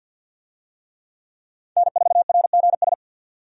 Morse code, heard as a keyed tone, about 51 wpm, 700 Hz. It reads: N4RGS
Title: USA Call Signs - 5 Characters - 50wpm